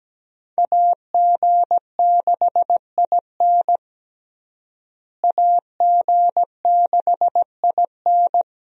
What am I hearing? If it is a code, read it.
Morse code, heard as a keyed tone, about 17 wpm, 700 Hz. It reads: AG6IN AG6IN
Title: USA Call Signs - 5 Characters - Repeated 3x - 17wpm